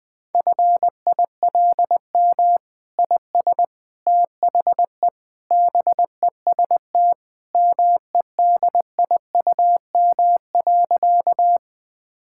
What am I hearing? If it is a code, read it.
Morse code, heard as a keyed tone, about 19 wpm, 700 Hz. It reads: FILM IS THE BEST MEDIUM.
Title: Sentences from Top 1000 Words - Review Part 6 - 20wpm